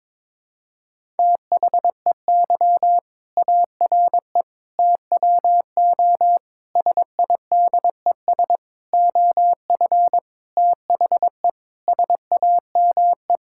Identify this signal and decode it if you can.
Morse code, heard as a keyed tone, about 22 wpm, 700 Hz. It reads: THEY ARE TWO SIDES OF THE SAME
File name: Sentences with 7-Word Phrases - 22wpm